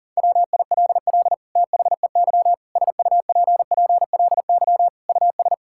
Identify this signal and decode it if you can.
Morse code, heard as a keyed tone, about 37 wpm, 700 Hz. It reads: WILL THEY SUPPLY US
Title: Sentences from Top 1000 Words - Review Part 5 - 40wpm